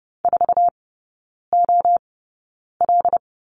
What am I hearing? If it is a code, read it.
Morse code, heard as a keyed tone, about 30 wpm, spaced as for 9 wpm, 700 Hz. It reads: 4OL
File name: 081 Lesson 14e - Number 2 - Three Characters 30wpm